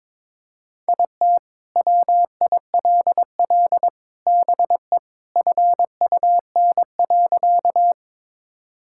Morse code, heard as a keyed tone, about 22 wpm, 700 Hz. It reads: IT WILL BE FUN.
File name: practice / 22wpm / Sentences with 3-Word Phrases - 22wpm